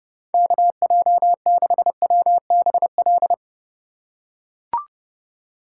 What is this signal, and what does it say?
Morse code, heard as a keyed tone, about 30 wpm, 700 Hz. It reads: KJ6WBL